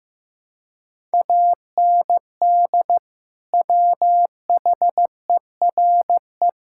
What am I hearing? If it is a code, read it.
Morse code, heard as a keyed tone, about 16 wpm, 700 Hz. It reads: AND WHERE
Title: Sentences from Top 1000 Words - Review Part 5 - 15wpm